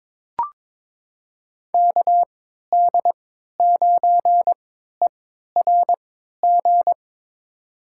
Morse code, heard as a keyed tone, about 21 wpm, spaced as for 12 wpm, 700 Hz. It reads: KD9ERG
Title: USA Call Signs - 6 Characters - 3x Character Spacing - 22wpm